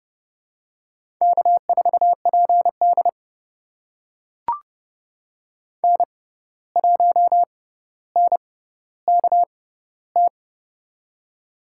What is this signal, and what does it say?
Morse code, heard as a keyed tone, about 31 wpm, spaced as for 15 wpm, 700 Hz. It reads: K4PD N1NKT
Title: USA Call Signs - 6x Character Spacing - 30wpm